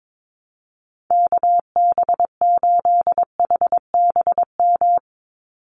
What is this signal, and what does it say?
Morse code, heard as a keyed tone, about 22 wpm, 700 Hz. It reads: KB8HBM
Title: USA Call Signs 22wpm